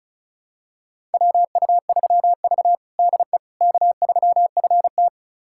Morse code, heard as a keyed tone, about 36 wpm, 700 Hz. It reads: WU3V DE K3FT
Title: USA Callsign DE Callsign - 4 Characters - 35wpm